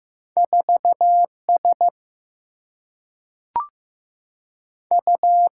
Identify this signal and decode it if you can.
Morse code, heard as a keyed tone, about 16 wpm, 700 Hz. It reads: 4 S U
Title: Single-Letter Single-Number 15wpm